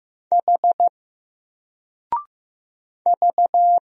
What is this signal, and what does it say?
Morse code, heard as a keyed tone, about 15 wpm, 700 Hz. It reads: H V
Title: Single Letter-Number 15wpm